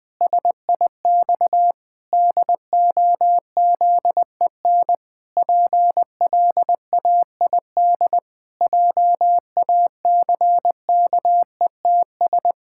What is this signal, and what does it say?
Morse code, heard as a keyed tone, about 20 wpm, 700 Hz. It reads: SIX DOZEN PLAID JACKETS
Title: Pangram Challenge v2 - 20wpm